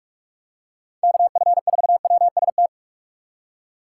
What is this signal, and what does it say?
Morse code, heard as a keyed tone, about 42 wpm, 700 Hz. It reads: KF4WST
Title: USA Call Signs - 6 Characters - 4x Character Spacing - 45wpm